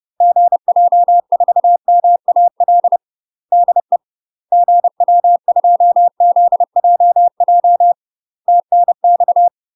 Morse code, heard as a keyed tone, about 30 wpm, 700 Hz. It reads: GJ4MAL DE GW2ZJJ TNX
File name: UK MASTER QSOs TRAINING / 30WPM / UK MASTER QSOs 0000